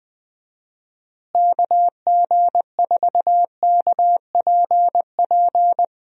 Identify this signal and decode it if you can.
Morse code, heard as a keyed tone, about 20 wpm, 700 Hz. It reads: KG4KPP
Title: USA Call Signs - 6 Characters - Repeated 3x - 20wpm